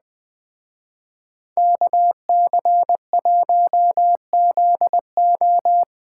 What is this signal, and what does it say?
Morse code, heard as a keyed tone, about 20 wpm, 700 Hz. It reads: KC1ZO
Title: Speed Racing - USA 2x2 Call Signs 17wpm